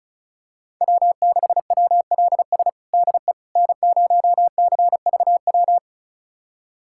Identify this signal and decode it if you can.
Morse code, heard as a keyed tone, about 36 wpm, 700 Hz. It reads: W6WLS DE N0CVW
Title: USA Callsign DE Callsign - 5 Characters - 35wpm